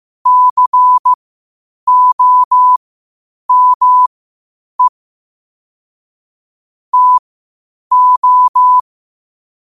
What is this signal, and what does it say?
Morse code, heard as a keyed tone, about 15 wpm, spaced as for 8 wpm, 1000 Hz. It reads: COME TO
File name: qotdfiles / 8wpm